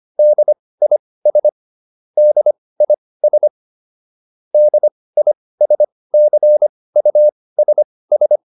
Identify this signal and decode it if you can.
Morse code, heard as a keyed tone, about 25 wpm, spaced as for 18 wpm, 600 Hz. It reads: DIS DIS DISCUSS